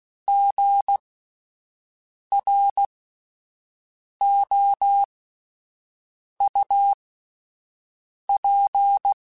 Morse code, heard as a keyed tone, about 16 wpm, spaced as for 5 wpm, 800 Hz. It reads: GROUP